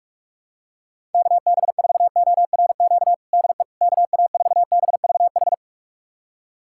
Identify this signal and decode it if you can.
Morse code, heard as a keyed tone, about 46 wpm, 700 Hz. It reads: KB4ORQ DE KA4BVH
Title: USA Callsign DE Callsign - 6 Characters - 45wpm